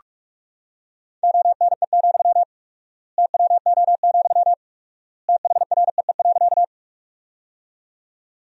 Morse code, heard as a keyed tone, about 46 wpm, 700 Hz. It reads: ONE, TWO, THREE.
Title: 4x Word-Spacing - Sentences from Top 200 Words - 45wpm